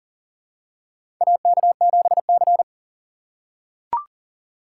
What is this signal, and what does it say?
Morse code, heard as a keyed tone, about 37 wpm, 700 Hz. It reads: AK7C
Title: USA Call Signs - 4 Characters - 4x Character Spacing - 40wpm